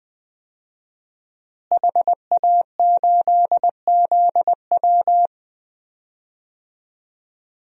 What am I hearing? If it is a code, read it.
Morse code, heard as a keyed tone, about 20 wpm, 700 Hz. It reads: HA8ZW
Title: DXCC-100 Call Signs - 5 Characters - Repeated 2x - 20wpm